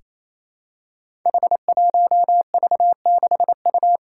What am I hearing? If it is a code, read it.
Morse code, heard as a keyed tone, about 28 wpm, 700 Hz. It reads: H1V6U
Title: Five Challenging Letter-Number Combinations 28wpm